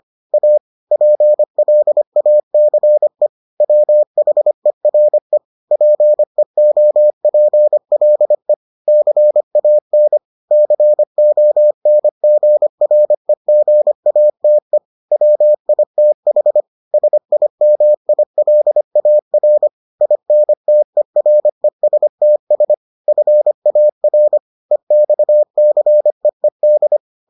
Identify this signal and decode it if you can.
Morse code, heard as a keyed tone, about 25 wpm, 600 Hz. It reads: A PLACE WHERE PEOPLE CAN CONGREGATE WITH SIMILAR INTERESTS FAR EXCEED